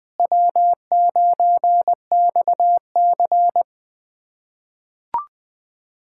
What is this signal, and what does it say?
Morse code, heard as a keyed tone, about 20 wpm, 700 Hz. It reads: W9XC
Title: K1USN SST QSO Practice - 20wpm